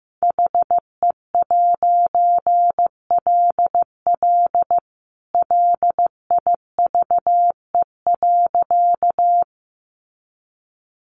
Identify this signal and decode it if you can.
Morse code, heard as a keyed tone, about 15 wpm, 700 Hz. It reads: HE'LL LIVE.
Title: A Sound Of Thunder 15wpm